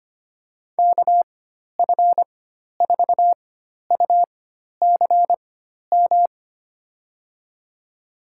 Morse code, heard as a keyed tone, about 25 wpm, spaced as for 12 wpm, 700 Hz. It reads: KF4UCM
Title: USA Call Signs - 4x Character Spacing - 25wpm